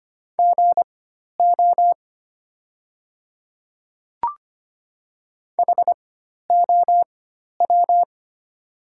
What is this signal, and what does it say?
Morse code, heard as a keyed tone, about 25 wpm, spaced as for 12 wpm, 700 Hz. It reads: GO HOW
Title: 4x Character Spacing - Top 200 Words - 25wpm